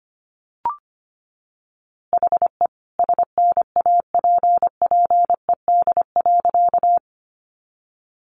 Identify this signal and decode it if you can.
Morse code, heard as a keyed tone, about 25 wpm, 700 Hz. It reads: HE SNAPPED.
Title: A Sound Of Thunder 25wpm